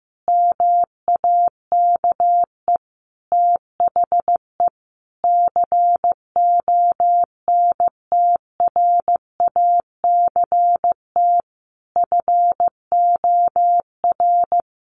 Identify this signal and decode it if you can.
Morse code, heard as a keyed tone, about 15 wpm, 700 Hz. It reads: MAKE THE CONTRACT FOR
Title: Sentences from Top 1400 Words 15wpm